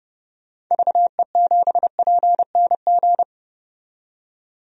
Morse code, heard as a keyed tone, about 30 wpm, 700 Hz. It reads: VE7PNG
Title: DXCC-100 Call Signs - 6 Characters - 30wpm